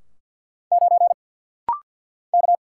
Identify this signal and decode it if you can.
Morse code, heard as a keyed tone, about 50 wpm, 700 Hz. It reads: 9 K
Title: Single Letter-Number - Rapid-Fire 50wpm